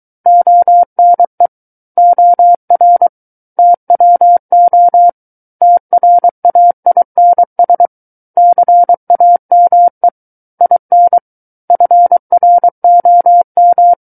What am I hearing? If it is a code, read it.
Morse code, heard as a keyed tone, about 23 wpm, 700 Hz. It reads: ONE OR TWO TRAINS CAME IN FROM
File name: Book One / 14-Chapter 14 - 23 WPM